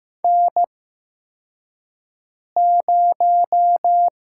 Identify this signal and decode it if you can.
Morse code, heard as a keyed tone, about 15 wpm, spaced as for 4 wpm, 700 Hz. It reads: N0
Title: USA Call Signs - 5 Characters - 8x Character Spacing - 15wpm